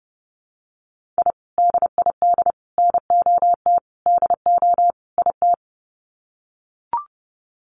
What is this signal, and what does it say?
Morse code, heard as a keyed tone, about 30 wpm, 700 Hz. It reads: I DID NOT DO IT
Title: Sets of 5 Words Encore2 30wpm